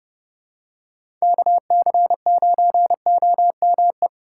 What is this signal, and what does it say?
Morse code, heard as a keyed tone, about 30 wpm, 700 Hz. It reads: KC9OME
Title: USA Call Signs - 2x Character Spacing - 30wpm